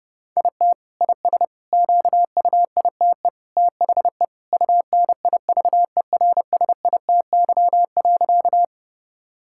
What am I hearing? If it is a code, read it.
Morse code, heard as a keyed tone, about 30 wpm, 700 Hz. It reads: IT IS QUITE THE UNIVERSITY.
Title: Sentences from Top 1000 Words - Review Part 5 - 30wpm